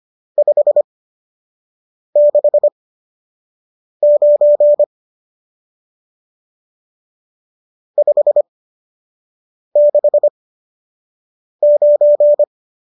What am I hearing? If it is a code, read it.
Morse code, heard as a keyed tone, about 25 wpm, spaced as for 6 wpm, 590 Hz. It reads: 569 569
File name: qso002 06